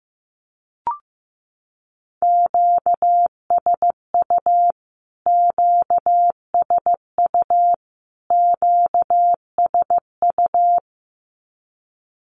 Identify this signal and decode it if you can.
Morse code, heard as a keyed tone, about 15 wpm, 700 Hz. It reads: QSU QSU QSU
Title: Q-Codes Repeated 3x 15wpm